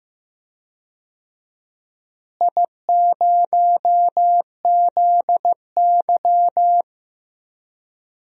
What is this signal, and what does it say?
Morse code, heard as a keyed tone, about 15 wpm, 700 Hz. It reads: I0ZY
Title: DXCC-100 Call Signs - 4 Characters - Repeated 3x - 15wpm